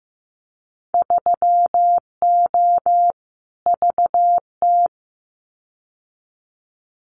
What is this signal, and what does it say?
Morse code, heard as a keyed tone, about 15 wpm, 700 Hz. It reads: 3O VT
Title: Winter Field Day Exchanges Encore 15wpm